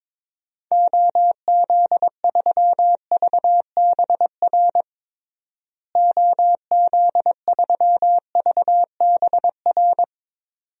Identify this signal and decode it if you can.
Morse code, heard as a keyed tone, about 22 wpm, 700 Hz. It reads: OZ3VBR OZ3VBR
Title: DXCC-100 Call Signs - 6 Characters - Repeated 2x - 22wpm